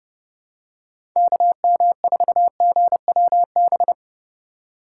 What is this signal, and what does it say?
Morse code, heard as a keyed tone, about 32 wpm, 700 Hz. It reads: KM4GWB
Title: USA Call Signs - 6 Characters - 8x Character Spacing - 30wpm